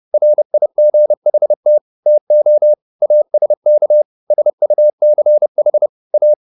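Morse code, heard as a keyed tone, about 30 wpm, 600 Hz. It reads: RIGHT TO ASK SUCH A
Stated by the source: Morse code at 30 wpm, the novel Never by Ken Follett in English, from CW med Gnister.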